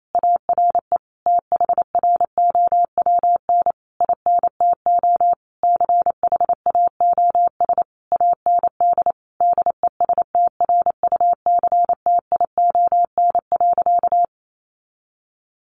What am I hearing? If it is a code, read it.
Morse code, heard as a keyed tone, about 28 wpm, 700 Hz. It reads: ARE THROWN INTO CHAOS AND DESTRUCTION.
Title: A Sound Of Thunder 28wpm